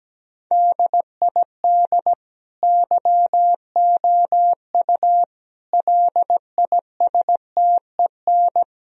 Morse code, heard as a keyed tone, about 17 wpm, 700 Hz. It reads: DID YOU LISTEN
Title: Sentences from Top 1000 Words - Review Part 4 - 17wpm